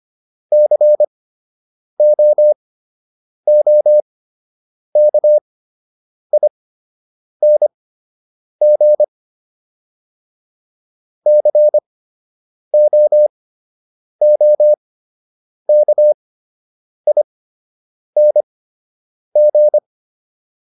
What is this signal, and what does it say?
Morse code, heard as a keyed tone, about 25 wpm, spaced as for 8 wpm, 600 Hz. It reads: COOKING COOKING